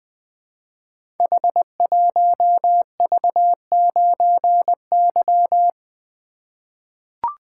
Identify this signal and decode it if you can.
Morse code, heard as a keyed tone, about 20 wpm, 700 Hz. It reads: H1V9Y
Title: Code Groups - 5 Letter:Numbers - 20wpm